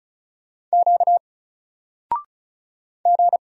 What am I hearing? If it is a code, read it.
Morse code, heard as a keyed tone, about 33 wpm, 700 Hz. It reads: Q G
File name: Single-Letter-Number-35wpm